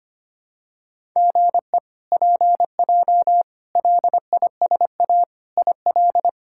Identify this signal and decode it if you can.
Morse code, heard as a keyed tone, about 25 wpm, 700 Hz. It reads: GE PJ LISA IL
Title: K1USN SST QSO Practice - 25wpm